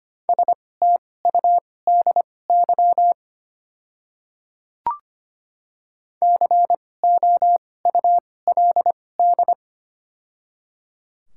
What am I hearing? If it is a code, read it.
Morse code, heard as a keyed tone, about 25 wpm, spaced as for 18 wpm, 700 Hz. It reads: STUDY COULD
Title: 2x Character Spacing - Top 500 Words - 25wpm